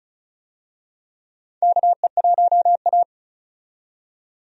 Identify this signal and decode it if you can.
Morse code, heard as a keyed tone, about 34 wpm, 700 Hz. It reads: KE1A
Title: USA Call Signs - 4 Characters - 35wpm